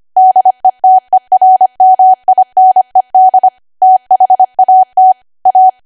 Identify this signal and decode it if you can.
Morse code, heard as a keyed tone, about 25 wpm, 750 Hz. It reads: DETERMINED THAT A
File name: Archive / 25 WPM / 25WPM